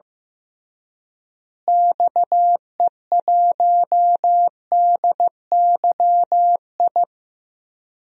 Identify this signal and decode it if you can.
Morse code, heard as a keyed tone, about 15 wpm, 700 Hz. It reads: XE1DYI
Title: DXCC-100 Call Signs - 6 Characters - Repeated 3x - 15wpm